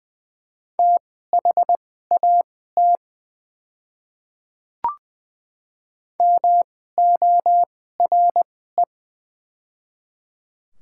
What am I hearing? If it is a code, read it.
Morse code, heard as a keyed tone, about 20 wpm, spaced as for 14 wpm, 700 Hz. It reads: THAT MORE
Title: 2x Character Spacing - Top 100 Words - 20wpm